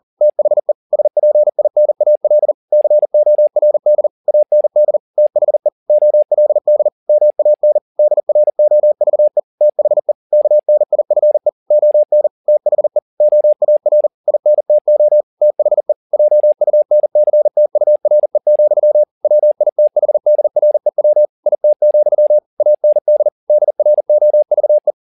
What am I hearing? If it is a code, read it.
Morse code, heard as a keyed tone, about 40 wpm, 600 Hz. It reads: THE SPINAL CORD AND THE OLD MAN DROVE THE KNIFE ON THE OAR INTO THE JUNCTURE, WITHDREW IT, AND DROVE
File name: the old man and the sea 2 40wpm